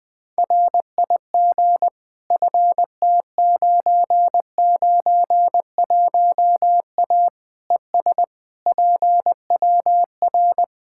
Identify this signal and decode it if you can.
Morse code, heard as a keyed tone, about 20 wpm, 700 Hz. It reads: RIG FT991A ES PWR
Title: Practice QSO - Exchange 2 - Info - 20wpm